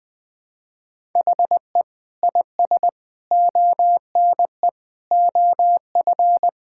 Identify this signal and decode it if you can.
Morse code, heard as a keyed tone, about 20 wpm, 700 Hz. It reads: HE IS ONE OF
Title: Sentences with 4-Word Phrases - 20wpm